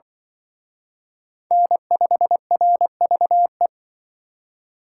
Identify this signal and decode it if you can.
Morse code, heard as a keyed tone, about 23 wpm, 700 Hz. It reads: N5RVE
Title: Speed Racing - USA 1x3 Call Signs 20wpm